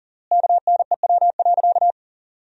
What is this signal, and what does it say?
Morse code, heard as a keyed tone, about 40 wpm, 700 Hz. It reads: KNEW.
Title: 3x Word-Spacing - Sentences from Top 400 Words - 40wpm